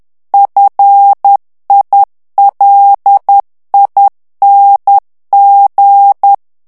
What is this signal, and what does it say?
Morse code, heard as a keyed tone, about 10 wpm, 800 Hz. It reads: FILING
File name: ANGLAIS texte clair 3 vitesse 10 mots minute